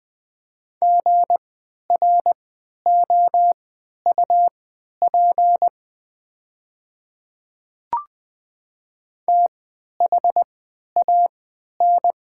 Morse code, heard as a keyed tone, about 21 wpm, spaced as for 12 wpm, 690 Hz. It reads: GROUP THAN
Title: 3x Character Spacing - Top 500 Words - 20wpm